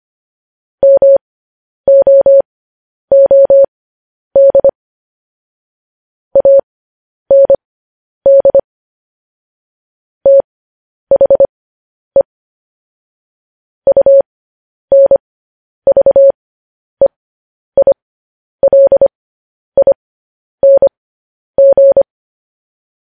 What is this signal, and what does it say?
Morse code, heard as a keyed tone, about 25 wpm, spaced as for 10 wpm, 550 Hz. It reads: MOOD AND THE UNVEILING
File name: podcast / News headlines-1x-25x10